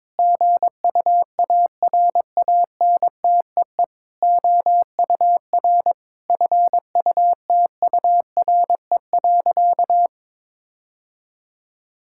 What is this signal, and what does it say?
Morse code, heard as a keyed tone, about 22 wpm, 700 Hz. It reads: GUARANTEE OUR FUTURE.
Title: Sentences from Top 1600 Words 22wpm